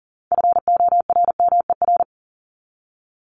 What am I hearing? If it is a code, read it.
Morse code, heard as a keyed tone, about 40 wpm, 700 Hz. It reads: FORMER
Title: Top 1751-2000 Words 40wpm